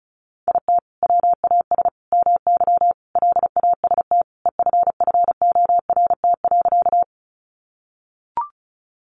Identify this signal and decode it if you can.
Morse code, heard as a keyed tone, about 34 wpm, 700 Hz. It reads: IT WAS MY LAST EFFORT.
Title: Sentences from Top 1200 Words 35wpm